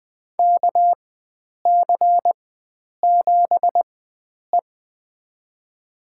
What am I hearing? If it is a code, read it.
Morse code, heard as a keyed tone, about 21 wpm, spaced as for 9 wpm, 700 Hz. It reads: KC7E
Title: USA Call Signs - 4x Character Spacing - 20wpm